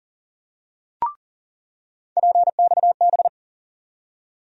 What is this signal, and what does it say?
Morse code, heard as a keyed tone, about 37 wpm, 700 Hz. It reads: PXB
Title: USA Call Sign Postfixes - 3 Letters 40wpm